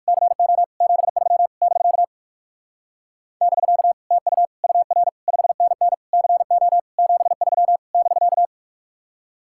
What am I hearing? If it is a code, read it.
Morse code, heard as a keyed tone, about 58 wpm, 700 Hz. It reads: CO 73 <BK> <BK> TU UR 5NN CO 73 <BK>